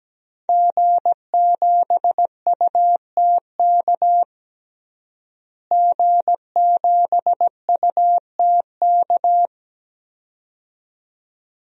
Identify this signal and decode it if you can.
Morse code, heard as a keyed tone, about 17 wpm, 700 Hz. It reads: G7UTK G7UTK
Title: DXCC-100 Call Signs - 5 Characters - Repeated 2x - 17wpm